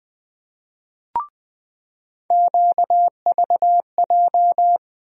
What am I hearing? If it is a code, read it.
Morse code, heard as a keyed tone, about 20 wpm, 700 Hz. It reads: QVJ
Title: USA Call Sign Postfixes - 3 Letters 20wpm